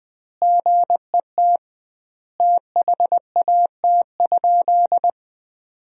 Morse code, heard as a keyed tone, about 20 wpm, 700 Hz. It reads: GET THAT?
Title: 2x Word-Spacing - Sentences from Top 200 Words - 20wpm